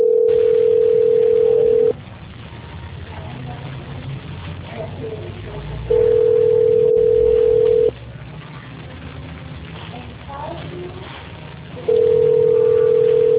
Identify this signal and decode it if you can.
Ringback tone (North America), 440 + 480 Hz, with speech.
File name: out- -2008- - - .29024